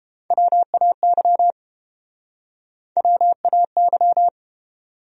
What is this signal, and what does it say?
Morse code, heard as a keyed tone, about 31 wpm, 700 Hz. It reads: WAY WAY